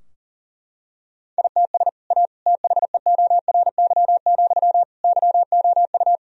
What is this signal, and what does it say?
Morse code, heard as a keyed tone, about 40 wpm, 700 Hz. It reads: ITS A THEORY, YOU
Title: Sentences from Top 1000 Words - Review Part 3 - 40wpm